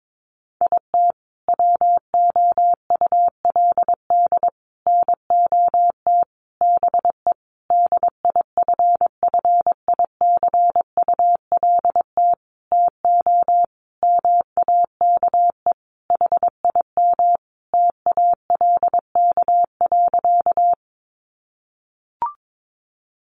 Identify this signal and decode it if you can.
Morse code, heard as a keyed tone, about 22 wpm, 700 Hz. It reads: IT WOULD NOT BE DIFFICULT TO MAKE HIM TALK.
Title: Sentences from Top 600 Words 22wpm